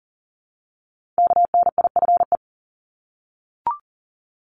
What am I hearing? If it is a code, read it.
Morse code, heard as a keyed tone, about 41 wpm, 700 Hz. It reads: KNIFE
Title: Multi-letter Phonemes Repeated 3x 40wpm